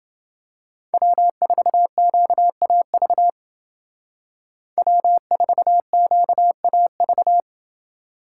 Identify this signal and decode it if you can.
Morse code, heard as a keyed tone, about 28 wpm, 700 Hz. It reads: W4QAV W4QAV